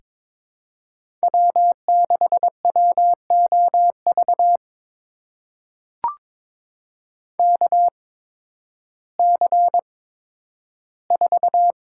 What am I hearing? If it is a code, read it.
Morse code, heard as a keyed tone, about 22 wpm, spaced as for 12 wpm, 700 Hz. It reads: W6WOV KC4
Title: USA Call Signs - 5 Characters - 8x Character Spacing - 22wpm